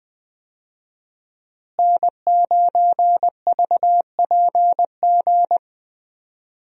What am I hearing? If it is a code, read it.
Morse code, heard as a keyed tone, about 21 wpm, 700 Hz. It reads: N9VPG
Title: USA Call Signs - 5 Characters - Repeated 2x - 20wpm